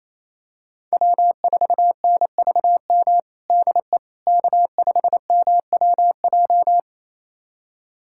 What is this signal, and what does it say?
Morse code, heard as a keyed tone, about 27 wpm, 700 Hz. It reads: W4NVM DE K5MWJ
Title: USA Callsign DE Callsign - 5 Characters - 28wpm